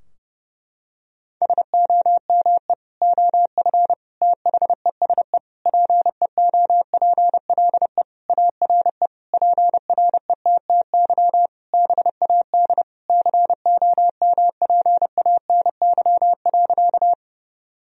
Morse code, heard as a keyed tone, about 30 wpm, 700 Hz. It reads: SOME OF THESE PEOPLE ARE PRETTY BAD COMPANY.
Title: Sentences from Top 1000 Words - Review Part 3 - 30wpm